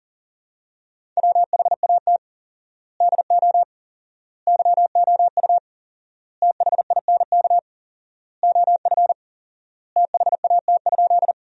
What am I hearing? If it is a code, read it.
Morse code, heard as a keyed tone, about 40 wpm, 700 Hz. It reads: WHAT DO YOU THINK OF THAT?
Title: 4x Word-Spacing - Sentences from Top 200 Words - 40wpm